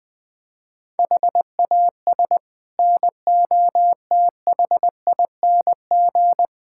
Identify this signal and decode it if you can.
Morse code, heard as a keyed tone, about 20 wpm, 700 Hz. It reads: HAS NOTHING
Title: Sets of 6 Words - 20wpm